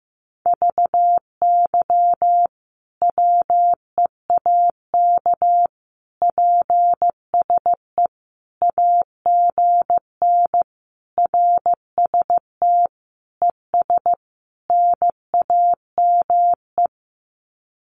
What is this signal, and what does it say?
Morse code, heard as a keyed tone, about 15 wpm, 700 Hz. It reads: VY WEAK PSE AGN RST ES NAME
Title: Common QSO Phrases 15wpm